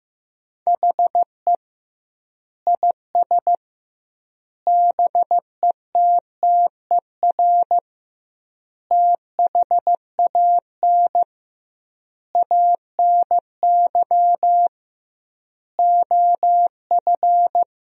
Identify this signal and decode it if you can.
Morse code, heard as a keyed tone, about 15 wpm, 700 Hz. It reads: HE IS BETTER THAN ANY OF
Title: 2x Word-Spacing - Sentences from Top 400 Words - 15wpm